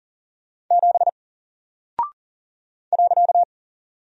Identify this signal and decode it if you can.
Morse code, heard as a keyed tone, about 40 wpm, 700 Hz. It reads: 7 .
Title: Single Character (Letter:Number:Punctuation:Prosign) Alt 40wpm